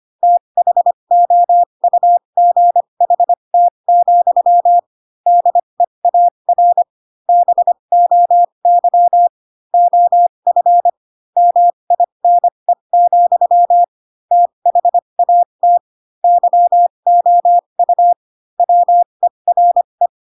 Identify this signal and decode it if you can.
Morse code, heard as a keyed tone, about 25 wpm, spaced as for 22 wpm, 700 Hz. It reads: THOUGHT, DEAR BOY OF MINE, THAT YOU WERE
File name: Pinocchio / Pinocchio ch7 18 25 ik1hge